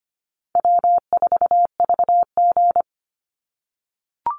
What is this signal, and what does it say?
Morse code, heard as a keyed tone, about 25 wpm, 700 Hz. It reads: W4VG
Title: Easy Call Signs - 25wpm